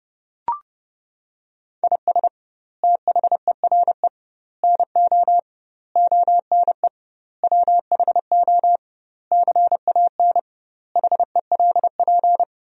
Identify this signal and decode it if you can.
Morse code, heard as a keyed tone, about 30 wpm, 700 Hz. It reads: IS THERE NO ONE WHO CAN HELP
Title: 2x Word-Spacing - Sentences from Top 200 Words - 30wpm